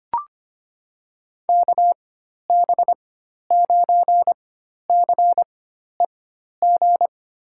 Morse code, heard as a keyed tone, about 25 wpm, spaced as for 12 wpm, 700 Hz. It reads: KB9CEG
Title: USA Call Signs - 6 Characters - 4x Character Spacing - 25wpm